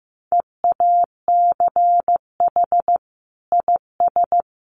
Morse code, heard as a keyed tone, about 15 wpm, 700 Hz. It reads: EACH IS
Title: Sentences from Top 1400 Words 15wpm